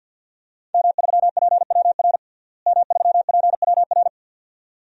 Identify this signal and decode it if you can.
Morse code, heard as a keyed tone, about 46 wpm, 700 Hz. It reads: M3PWR M3PWR
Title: DXCC-100 Call Signs - 5 Characters - Repeated 2x - 50wpm